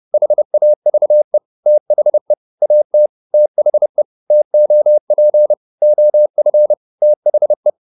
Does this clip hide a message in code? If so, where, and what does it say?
Yes: Morse code from 0.1 to 7.7 s, heard as a keyed tone, about 30 wpm, 600 Hz. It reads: HAVE THE AT THE TOP OF THE